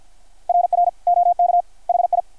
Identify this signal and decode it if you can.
Morse code, heard as a keyed tone, about 48 wpm, 680 Hz. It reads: RR OK HI
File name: archived / CRSnet